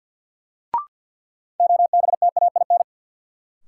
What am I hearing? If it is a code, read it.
Morse code, heard as a keyed tone, about 53 wpm, 700 Hz. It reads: OBTAIN
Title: Sentences from Top 1300 Words - New Word Drill Alt - 50wpm